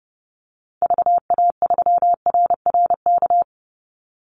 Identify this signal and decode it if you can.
Morse code, heard as a keyed tone, about 30 wpm, 700 Hz. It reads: VA3RRK
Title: Call Signs - 30wpm